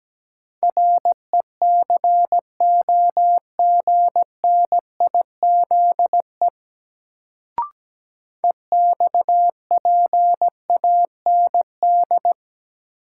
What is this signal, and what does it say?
Morse code, heard as a keyed tone, about 17 wpm, 700 Hz. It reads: RECOGNIZE EXPAND
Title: Sentences from Top 1400 Words - New Word Drill - 17wpm